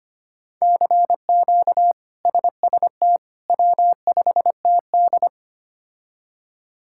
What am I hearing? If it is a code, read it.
Morse code, heard as a keyed tone, about 25 wpm, 700 Hz. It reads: CQ SST W5TD
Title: K1USN SST QSO Practice - 25wpm